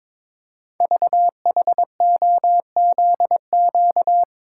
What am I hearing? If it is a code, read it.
Morse code, heard as a keyed tone, about 23 wpm, 700 Hz. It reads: VHOZQ
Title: Code Groups - 5 Letters - 22wpm